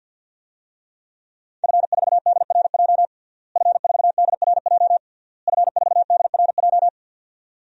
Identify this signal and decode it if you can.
Morse code, heard as a keyed tone, about 52 wpm, 700 Hz. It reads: F4DRJ F4DRJ F4DRJ
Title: DXCC-100 Call Signs - 5 Characters - Repeated 3x - 50wpm